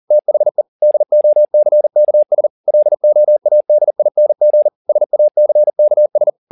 Morse code, heard as a keyed tone, about 40 wpm, 600 Hz. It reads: THE DOCKS LOADING SACKS
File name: the old man and the sea 2 40wpm